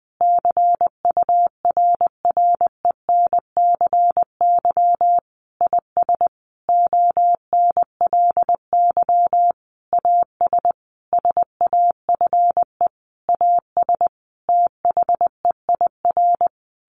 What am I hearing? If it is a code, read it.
Morse code, heard as a keyed tone, about 20 wpm, 700 Hz. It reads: CURRENCY IS ONLY AS SAFE AS THEIR
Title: Sentences from Top 1500 Words 20wpm